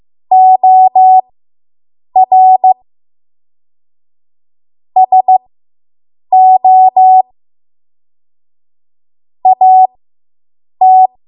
Morse code, heard as a keyed tone, about 15 wpm, spaced as for 7 wpm, 750 Hz. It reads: OR SO AT